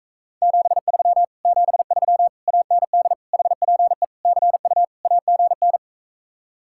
Morse code, heard as a keyed tone, about 44 wpm, 700 Hz. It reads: 73 73 AND HPE CU AGN